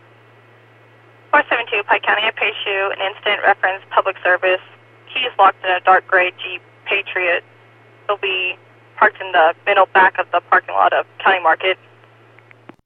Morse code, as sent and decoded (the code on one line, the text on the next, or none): none